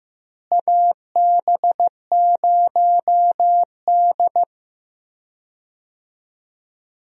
Morse code, heard as a keyed tone, about 15 wpm, 700 Hz. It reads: AB0D
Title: USA Call Signs - 4 Characters - Repeated 2x - 15wpm